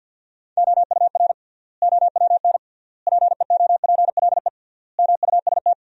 Morse code, heard as a keyed tone, about 50 wpm, 700 Hz. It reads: OUR OWN PEOPLE MUST
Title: 3x Word-Spacing - Sentences from Top 300 Words - 50wpm